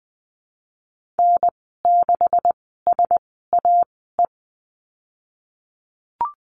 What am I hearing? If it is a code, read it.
Morse code, heard as a keyed tone, about 20 wpm, spaced as for 15 wpm, 700 Hz. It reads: N6SAE
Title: 128 Lesson 21g - Number 6 - Call Signs 20wpm